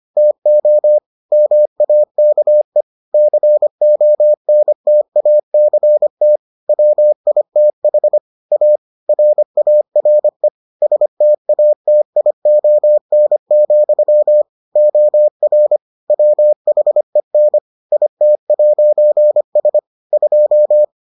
Morse code, heard as a keyed tone, about 25 wpm, 600 Hz. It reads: TO MAKE CONTACT WITH A RARE STATION, OR WHEN IT'S 2